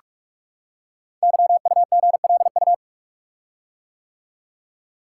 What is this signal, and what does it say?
Morse code, heard as a keyed tone, about 44 wpm, 700 Hz. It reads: YUGLU
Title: Code Groups - 5 Letter:Numbers - 45wpm